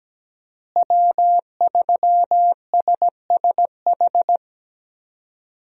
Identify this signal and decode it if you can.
Morse code, heard as a keyed tone, about 17 wpm, 700 Hz. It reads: W3SSH
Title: USA Call Signs - 5 Characters - Repeated 2x - 17wpm